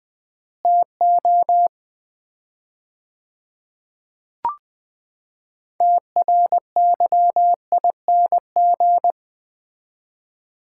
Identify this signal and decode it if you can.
Morse code, heard as a keyed tone, about 20 wpm, 700 Hz. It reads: TO TRYING
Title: 4x Word-Spacing - Sets of 4 Words - 20wpm